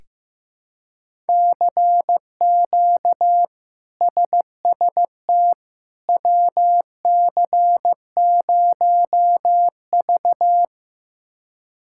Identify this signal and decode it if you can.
Morse code, heard as a keyed tone, about 15 wpm, 700 Hz. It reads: CQ SST WC0V
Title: K1USN SST QSO Practice - 15wpm